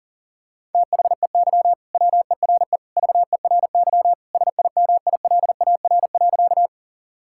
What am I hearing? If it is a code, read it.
Morse code, heard as a keyed tone, about 37 wpm, 700 Hz. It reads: THEY WERE VERY SIMILAR.
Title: Sentences from Top 1000 Words - Review Part 5 - 40wpm